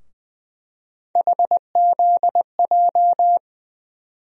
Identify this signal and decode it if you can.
Morse code, heard as a keyed tone, about 20 wpm, 700 Hz. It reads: HZJ
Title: USA Call Sign Postfixes - 3 Letters 20wpm